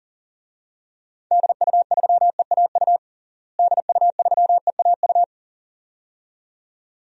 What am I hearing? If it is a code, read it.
Morse code, heard as a keyed tone, about 40 wpm, 700 Hz. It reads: DU3EAU DU3EAU
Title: DXCC-100 Call Signs - 6 Characters - Repeated 2x - 40wpm